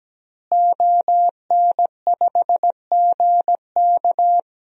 Morse code, heard as a keyed tone, about 17 wpm, 700 Hz. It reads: ON5GK
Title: DXCC-100 Call Signs - 5 Characters - Repeated 3x - 17wpm